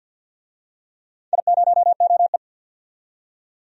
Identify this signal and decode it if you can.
Morse code, heard as a keyed tone, about 52 wpm, 700 Hz. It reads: I0OE
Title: DXCC-100 Call Signs - 4 Characters - 50wpm